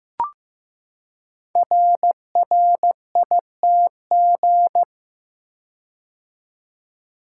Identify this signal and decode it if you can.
Morse code, heard as a keyed tone, about 15 wpm, 700 Hz. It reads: RRITG
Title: Code Groups - 5 Letter:Numbers - 15wpm